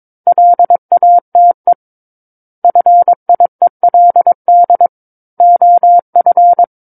Morse code, heard as a keyed tone, about 22 wpm, 700 Hz. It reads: LATE FIELD OF
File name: Book One / 13-Chapter 13 - 22 WPM